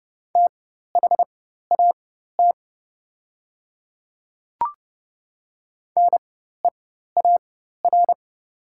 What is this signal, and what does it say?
Morse code, heard as a keyed tone, about 30 wpm, spaced as for 14 wpm, 700 Hz. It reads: THAT NEAR
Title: 4x Character Spacing - Top 200 Words - 30wpm